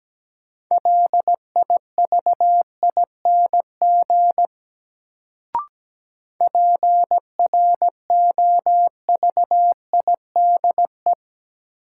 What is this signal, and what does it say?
Morse code, heard as a keyed tone, about 17 wpm, 700 Hz. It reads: LIVING PROVIDE
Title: Sentences from Top 1100 Words - New Word Drill Alt - 17wpm